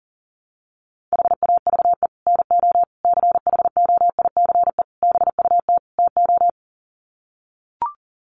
Morse code, heard as a keyed tone, about 40 wpm, 700 Hz. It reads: HAVE NO CHOICE BUT TO
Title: Sets of 5 Words Encore 40wpm